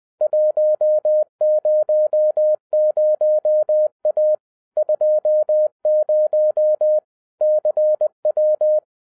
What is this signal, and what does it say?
Morse code, heard as a keyed tone, about 20 wpm, 610 Hz. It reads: 100A 20 CW
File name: 2011-07-mp3 / n23 QRP OTAKU 20wpm